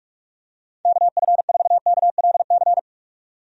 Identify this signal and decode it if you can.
Morse code, heard as a keyed tone, about 46 wpm, 700 Hz. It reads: KF4KLC
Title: USA Call Signs - 6 Characters - 45wpm